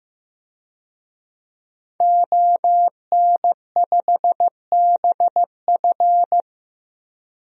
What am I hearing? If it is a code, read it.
Morse code, heard as a keyed tone, about 15 wpm, 700 Hz. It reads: ON5BF
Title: DXCC-100 Call Signs - 5 Characters - 15wpm